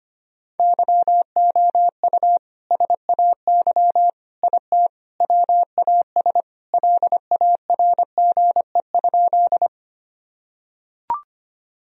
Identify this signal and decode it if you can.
Morse code, heard as a keyed tone, about 25 wpm, 700 Hz. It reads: YOU SAY IT WAS LARGE?
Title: Sentences from Top 1000 Words - Review Part 1 - 25wpm